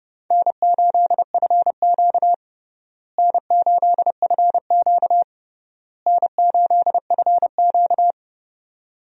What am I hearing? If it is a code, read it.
Morse code, heard as a keyed tone, about 30 wpm, 700 Hz. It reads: N8FQ N8FQ N8FQ
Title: USA Call Signs - 4 Characters - Repeated 3x - 30wpm